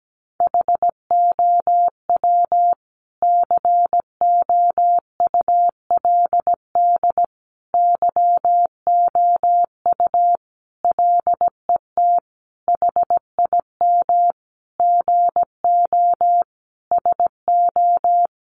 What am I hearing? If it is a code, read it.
Morse code, heard as a keyed tone, about 17 wpm, 700 Hz. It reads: HOW COULD YOU LET HIM GO SO
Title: Sentences from Top 300 Words 17wpm